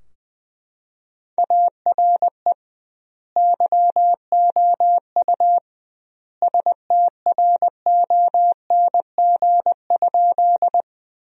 Morse code, heard as a keyed tone, about 21 wpm, 700 Hz. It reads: ARE YOU STRONG?
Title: 2x Word-Spacing - Sentences from Top 500 Words - 20wpm